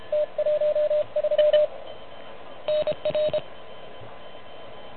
Morse code, heard as a keyed tone, about 31 wpm, 600 Hz, with radio noise. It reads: T13 NR
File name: truSDX / MWC29.8